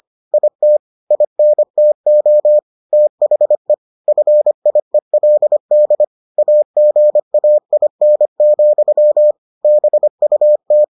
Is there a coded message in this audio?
Yes: Morse code from 0.3 to 10.8 s, heard as a keyed tone, about 25 wpm, 600 Hz. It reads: IT INTO THE FIELD AGAIN, BUT